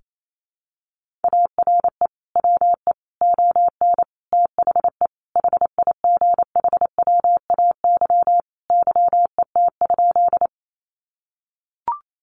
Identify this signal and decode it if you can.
Morse code, heard as a keyed tone, about 28 wpm, 700 Hz. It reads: ARE WE ON THE HIGHWAY YET?
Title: Sentences from Top 1400 Words 28wpm